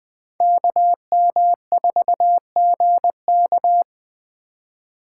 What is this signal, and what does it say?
Morse code, heard as a keyed tone, about 21 wpm, 700 Hz. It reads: KM4GK
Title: Speed Racing - USA Call Signs 17wpm